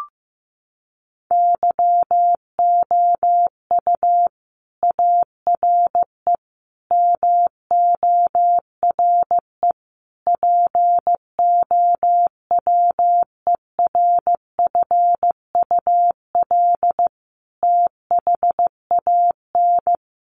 Morse code, heard as a keyed tone, about 15 wpm, 700 Hz. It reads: YOU ARE MORE POWERFUL THAN
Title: Sentences from Top 1300 Words 15wpm